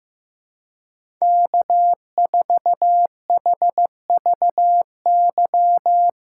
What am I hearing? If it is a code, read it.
Morse code, heard as a keyed tone, about 15 wpm, 700 Hz. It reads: K4HVY
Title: USA Call Signs - 5 Characters - 4x Character Spacing - 15wpm